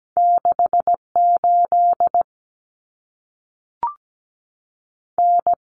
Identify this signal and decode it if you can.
Morse code, heard as a keyed tone, about 17 wpm, 700 Hz. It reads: 68 N
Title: Common Cut Numbers - 2-Digit 17wpm